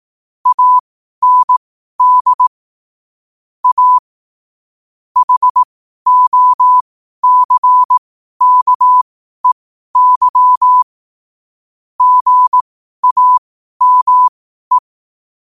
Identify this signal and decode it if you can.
Morse code, heard as a keyed tone, about 18 wpm, spaced as for 12 wpm, 1000 Hz. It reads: AND A HOCKEY GAME
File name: qotdfiles / 12wpm